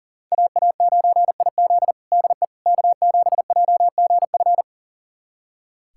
Morse code, heard as a keyed tone, about 40 wpm, 700 Hz. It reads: AA9IZ DE K7JGF
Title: USA Callsign DE Callsign - 5 Characters - 40wpm